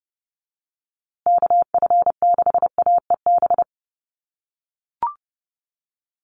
Morse code, heard as a keyed tone, about 30 wpm, 700 Hz. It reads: KF6AEB
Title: USA Call Signs Repeated 3x 30wpm